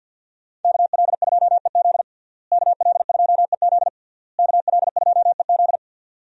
Morse code, heard as a keyed tone, about 52 wpm, 700 Hz. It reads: KL2EZ KL2EZ KL2EZ
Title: USA Call Signs - 5 Characters - Repeated 3x - 50wpm